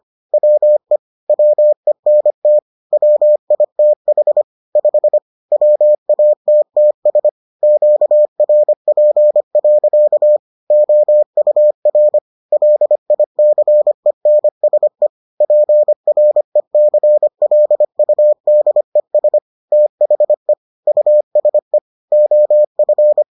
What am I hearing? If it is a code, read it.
Morse code, heard as a keyed tone, about 25 wpm, 600 Hz. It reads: WE WENT WITH 5 WATTS QRP. OUR LICENSE PRECLUDES THE USE OF